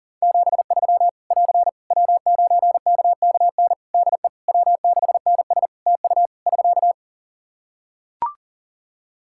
Morse code, heard as a keyed tone, about 40 wpm, 700 Hz. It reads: 73 <AR> W9KKN DE W6NS TU <SK>
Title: Practice QSO - Exchange 4 - Ending - 40wpm